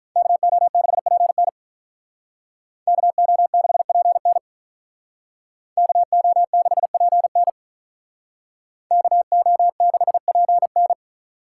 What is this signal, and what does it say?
Morse code, heard as a keyed tone, about 42 wpm, 700 Hz. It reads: KO6PN KO6PN KO6PN KO6PN
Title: Speed Racing - USA 2x2 Call Signs 35wpm